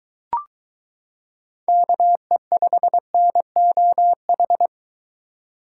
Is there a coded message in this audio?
Yes: Morse code from 1.7 to 4.7 s, heard as a keyed tone, about 23 wpm, 700 Hz. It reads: KE5NOH